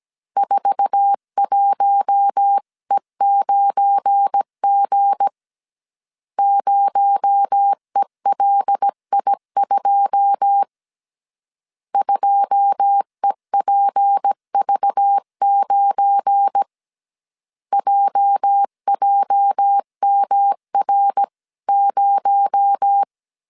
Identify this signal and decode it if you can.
Morse code, heard as a keyed tone, about 17 wpm, 800 Hz. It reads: 41E9G 0ELI2 2EPV9 JJMR0
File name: CW / DET / 1 9 1500C S C 16wpm 2909